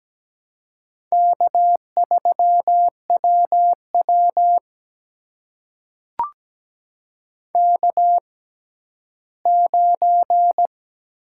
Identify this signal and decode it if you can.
Morse code, heard as a keyed tone, about 17 wpm, spaced as for 11 wpm, 700 Hz. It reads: K3WW K9
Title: USA Call Signs - 4 Characters - 6x Character Spacing - 17wpm